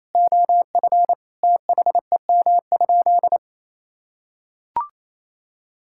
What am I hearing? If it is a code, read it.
Morse code, heard as a keyed tone, about 28 wpm, 700 Hz. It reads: OF THEM?
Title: Sentences from Top 1000 Words - Review Part 6 - 28wpm